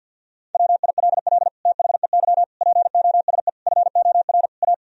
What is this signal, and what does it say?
Morse code, heard as a keyed tone, about 48 wpm, 700 Hz. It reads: WILL THEY POSE FOR A
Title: Sentences from Top 1000 Words - Review Part 2 - 50wpm